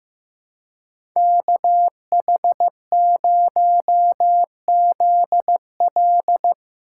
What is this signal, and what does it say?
Morse code, heard as a keyed tone, about 16 wpm, 700 Hz. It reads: KH0ZL
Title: USA Call Signs - 5 Characters - 2x Character Spacing - 15wpm